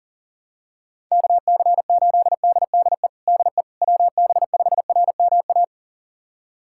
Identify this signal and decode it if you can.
Morse code, heard as a keyed tone, about 40 wpm, 690 Hz. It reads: KC8DDE DE WB5RMA
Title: USA Callsign DE Callsign - 6 Characters - 40wpm